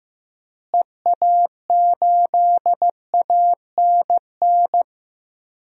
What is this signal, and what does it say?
Morse code, heard as a keyed tone, about 15 wpm, 700 Hz. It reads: EA8ANN
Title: DXCC-100 Call Signs - 6 Characters - Repeated 2x - 15wpm